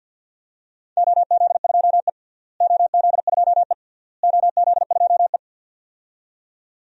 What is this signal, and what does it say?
Morse code, heard as a keyed tone, about 53 wpm, 700 Hz. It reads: OZ2E OZ2E OZ2E
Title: DXCC-100 Call Signs - 4 Characters - Repeated 3x - 50wpm